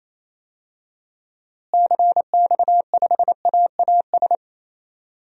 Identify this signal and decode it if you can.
Morse code, heard as a keyed tone, about 28 wpm, 700 Hz. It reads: CX5AAS
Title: DXCC-100 Call Signs - 6 Characters - Repeated 3x - 28wpm